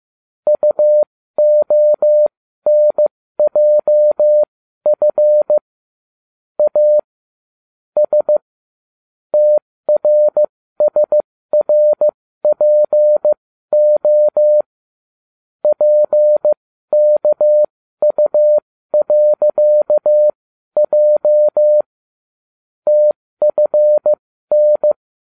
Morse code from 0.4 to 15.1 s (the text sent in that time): UONJF A S TRSRPO